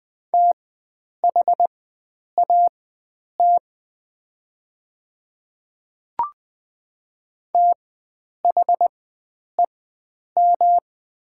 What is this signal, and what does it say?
Morse code, heard as a keyed tone, about 20 wpm, spaced as for 9 wpm, 700 Hz. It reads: THAT THEM
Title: 4x Character Spacing - Top 100 Words - 20wpm